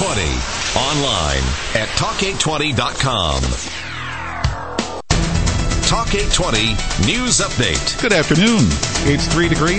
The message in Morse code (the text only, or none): none